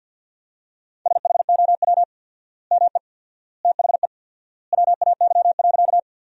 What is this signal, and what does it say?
Morse code, heard as a keyed tone, about 46 wpm, 700 Hz. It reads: SHOW ME THE WAY.
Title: 4x Word-Spacing - Sentences from Top 200 Words - 50wpm